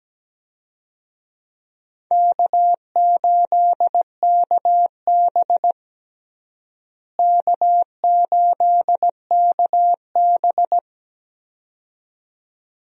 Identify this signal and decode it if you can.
Morse code, heard as a keyed tone, about 17 wpm, 700 Hz. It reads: K8KB K8KB
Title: USA Call Signs - 4 Characters - Repeated 2x - 17wpm